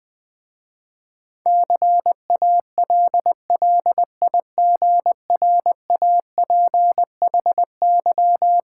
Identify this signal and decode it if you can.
Morse code, heard as a keyed tone, about 20 wpm, 700 Hz. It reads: CALLIGRAPHY
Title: Difficult Words with Hints - 20wpm